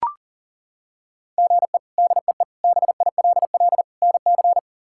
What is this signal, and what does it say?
Morse code, heard as a keyed tone, about 40 wpm, 700 Hz. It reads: GE DEE BILL NC
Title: K1USN SST QSO Practice - 40wpm